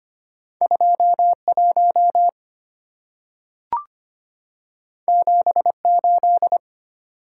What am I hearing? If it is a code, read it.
Morse code, heard as a keyed tone, about 25 wpm, 690 Hz. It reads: 21 78
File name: Two Letter-Number Combinations - 25wpm